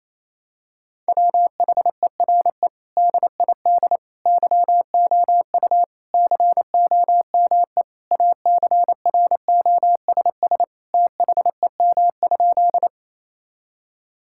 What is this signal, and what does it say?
Morse code, heard as a keyed tone, about 28 wpm, 700 Hz. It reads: WHERE DID YOU COME ACROSS THEM?
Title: Sentences from Top 1000 Words - Review Part 5 - 28wpm